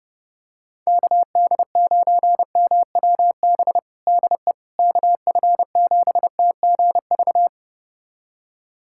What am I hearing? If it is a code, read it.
Morse code, heard as a keyed tone, about 30 wpm, 700 Hz. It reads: KD9MWB DE KF7TGV
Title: USA Callsign DE Callsign - 6 Characters - 30wpm